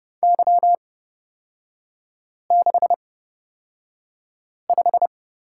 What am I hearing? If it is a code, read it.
Morse code, heard as a keyed tone, about 30 wpm, spaced as for 5 wpm, 700 Hz. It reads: Y65
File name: Lesson 36e - BT - Three Characters 30wpm